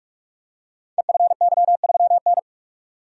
Morse code, heard as a keyed tone, about 44 wpm, 700 Hz. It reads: EFY3N
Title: Code Groups - 5 Letter:Numbers - 45wpm